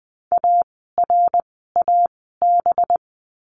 Morse code, heard as a keyed tone, about 20 wpm, spaced as for 15 wpm, 700 Hz. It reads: ARAB
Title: 185 Lesson 30f - Letter b - Words 20wpm